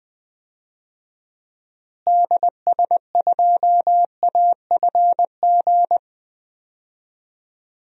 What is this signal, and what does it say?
Morse code, heard as a keyed tone, about 20 wpm, 700 Hz. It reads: DS2AFG
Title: DXCC-100 Call Signs - 6 Characters - Repeated 2x - 20wpm